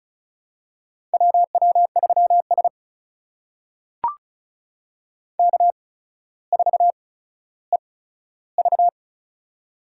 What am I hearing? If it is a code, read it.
Morse code, heard as a keyed tone, about 36 wpm, spaced as for 15 wpm, 700 Hz. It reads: WW3S K4EV
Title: USA Call Signs - 4 Characters - 8x Character Spacing - 35wpm